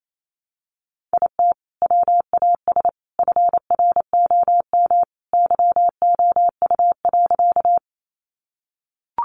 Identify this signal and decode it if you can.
Morse code, heard as a keyed tone, about 28 wpm, 700 Hz. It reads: IT WAS FROM YOU.
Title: Sentences from Top 100 Words 28wpm